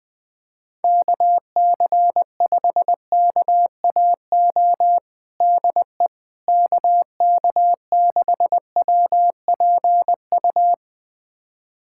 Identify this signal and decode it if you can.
Morse code, heard as a keyed tone, about 20 wpm, 700 Hz. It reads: KC5KAO DE KK6WPU
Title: USA Callsign DE Callsign 20wpm